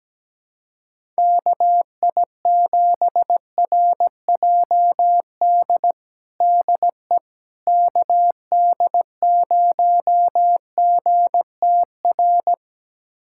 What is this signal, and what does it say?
Morse code, heard as a keyed tone, about 17 wpm, 700 Hz. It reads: KI7RJD DE KD0GTR
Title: USA Callsign DE Callsign - 6 Characters - 17wpm